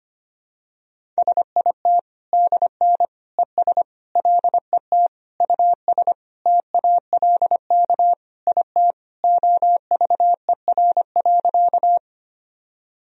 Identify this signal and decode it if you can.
Morse code, heard as a keyed tone, about 25 wpm, 700 Hz. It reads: SIT DN ES LET US TALK IT OVER.
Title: Sentences from Top 500 Words and Abbreviations 25wpm